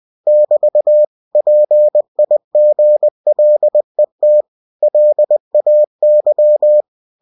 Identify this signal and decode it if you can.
Morse code, heard as a keyed tone, about 20 wpm, 600 Hz. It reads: <BT> PIGLET LAY